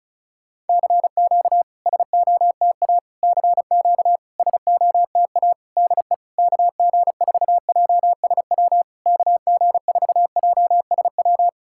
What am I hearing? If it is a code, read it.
Morse code, heard as a keyed tone, about 36 wpm, 690 Hz. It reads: CQ SOTA CQ SOTA DE KG4JSW KG4JSW
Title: SOTA QSO Practice 35wpm